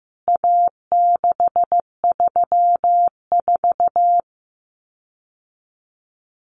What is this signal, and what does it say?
Morse code, heard as a keyed tone, about 15 wpm, 700 Hz. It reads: A634
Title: CWT Exchange 15wpm